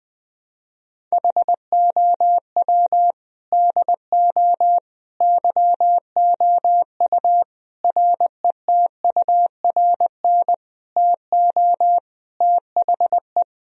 Morse code, heard as a keyed tone, about 20 wpm, 700 Hz. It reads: HOW DO YOU RETURN TO THE
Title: Sentences from Top 1000 Words - Review Part 3 - 20wpm